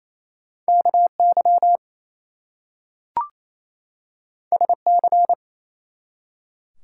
Morse code, heard as a keyed tone, about 27 wpm, 700 Hz. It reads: KY SC
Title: US States and Canadian Provinces 28wpm